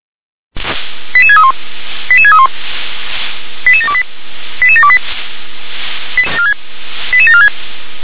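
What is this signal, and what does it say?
Five-tone selective call (ZVEI), heard as quick runs of five tones, tones of about 70 ms. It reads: 88631 88631 88728 88728 88746
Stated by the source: ZVEI five-tone selective call received on an SDR.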